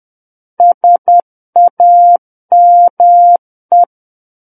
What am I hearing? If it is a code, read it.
Morse code, heard as a keyed tone, about 10 wpm, 700 Hz. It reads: SAME
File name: Book One / 01-Chapter 1 - 10 WPM